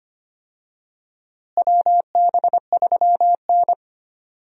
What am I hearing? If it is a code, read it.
Morse code, heard as a keyed tone, about 25 wpm, 700 Hz. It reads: WB3N
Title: USA Call Signs - 4 Characters - Repeated 2x - 25wpm